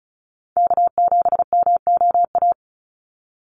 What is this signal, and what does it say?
Morse code, heard as a keyed tone, about 34 wpm, 700 Hz. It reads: K7MOA
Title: Call Signs Repeated 3x - 35wpm